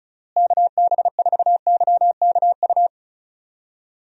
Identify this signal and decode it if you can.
Morse code, heard as a keyed tone, about 34 wpm, 700 Hz. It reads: KB4YKU
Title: USA Call Signs - 6 Characters - Repeated 2x - 35wpm